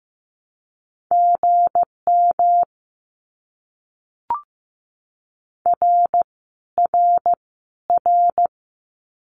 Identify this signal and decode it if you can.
Morse code, heard as a keyed tone, about 15 wpm, 700 Hz. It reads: GM R R R
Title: Abbreviations Repeated 3x - 15wpm